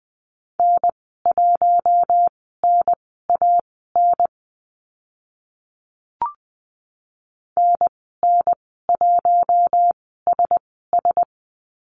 Morse code, heard as a keyed tone, about 20 wpm, spaced as for 14 wpm, 700 Hz. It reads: N1NAN NN1SS
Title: 042 Lesson 8g - Number 1 - Call Signs 20wpm